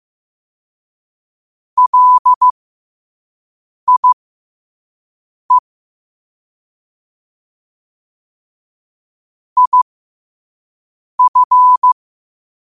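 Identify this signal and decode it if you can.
Morse code, heard as a keyed tone, about 15 wpm, spaced as for 5 wpm, 1000 Hz. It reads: LIE IF
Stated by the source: Morse practice quote sent at 5 wpm.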